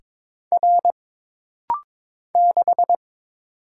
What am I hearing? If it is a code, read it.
Morse code, heard as a keyed tone, about 23 wpm, 700 Hz. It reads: R 6
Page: This rapid-fire timing is geared to push your ICR (Instant Character Recognition) skills.